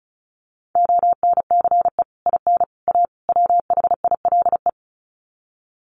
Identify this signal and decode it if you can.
Morse code, heard as a keyed tone, about 34 wpm, 700 Hz. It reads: ONCE IN A WHILE
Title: Sets of 4 Words 35wpm